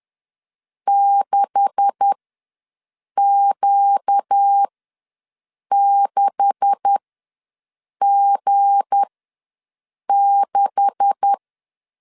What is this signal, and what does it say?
Morse code, heard as a keyed tone, about 10 wpm, spaced as for 6 wpm, 800 Hz. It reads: 6Q6G6
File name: LEZIONI / 9 286 ws 10wpm 4032